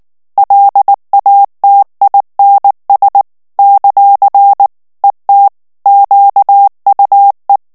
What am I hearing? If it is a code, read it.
Morse code, heard as a keyed tone, about 19 wpm, 800 Hz. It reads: LATINS ; ET QUE
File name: CW / FICHIERS-MP3 / FRANCAIS TEXTE CLAIR 2 VITESSE 18 MOTS MINUTES